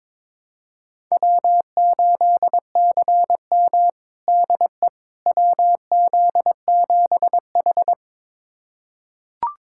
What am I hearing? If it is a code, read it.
Morse code, heard as a keyed tone, about 22 wpm, 700 Hz. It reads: W8CM DE WZ7H
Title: USA Callsign DE Callsign - 4 Characters - 22wpm